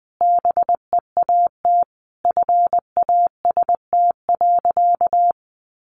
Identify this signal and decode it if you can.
Morse code, heard as a keyed tone, about 19 wpm, 700 Hz. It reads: BEAT FAST.
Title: Sentences from Top 1300 Words 20wpm